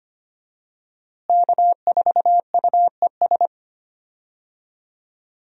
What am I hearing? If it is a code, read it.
Morse code, heard as a keyed tone, about 25 wpm, 700 Hz. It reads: K4UES
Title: USA Call Signs - 5 Characters - Repeated 3x - 25wpm